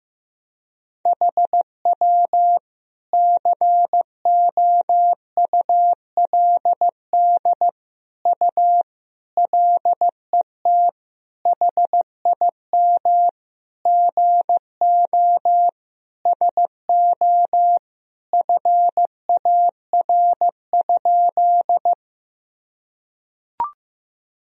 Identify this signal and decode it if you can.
Morse code, heard as a keyed tone, about 15 wpm, 700 Hz. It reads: HW COULD U LET HIM GO SO FAR?
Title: Sentences from Top 300 Words and Abbreviations 15wpm